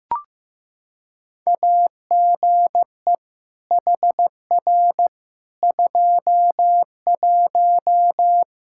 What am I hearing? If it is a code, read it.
Morse code, heard as a keyed tone, about 15 wpm, 700 Hz. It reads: AGE HR 21
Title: K1USN SST QSO Practice 15wpm